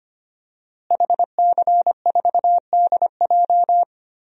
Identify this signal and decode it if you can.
Morse code, heard as a keyed tone, about 25 wpm, 700 Hz. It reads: HC4DJ
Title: DXCC-100 Call Signs - 5 Characters - Repeated 3x - 25wpm